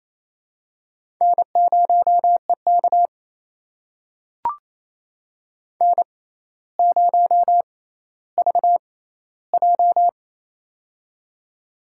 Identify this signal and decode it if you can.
Morse code, heard as a keyed tone, about 28 wpm, spaced as for 14 wpm, 700 Hz. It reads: N0EK N0VJ
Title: USA Call Signs - 4 Characters - 6x Character Spacing - 28wpm